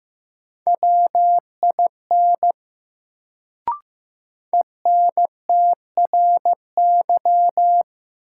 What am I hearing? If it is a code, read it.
Morse code, heard as a keyed tone, about 15 wpm, 700 Hz. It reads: WIN ENTRY
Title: Sentences from Top 1500 Words - New Word Drill - 15wpm